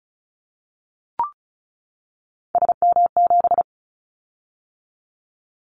Morse code, heard as a keyed tone, about 36 wpm, 700 Hz. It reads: SM7
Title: Callsign Prefixes for Top 25 DXCC - 35wpm